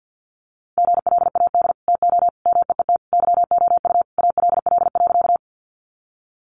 Morse code, heard as a keyed tone, about 50 wpm, 700 Hz. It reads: GLAD TO MEET YOU ALL.
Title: Sentences from Top 1100 Words 50wpm